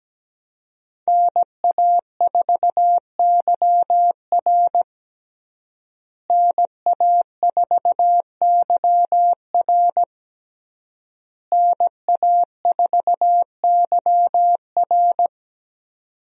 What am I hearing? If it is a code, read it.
Morse code, heard as a keyed tone, about 17 wpm, 700 Hz. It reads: NA4YR NA4YR NA4YR
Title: USA Call Signs - 5 Characters - Repeated 3x - 17wpm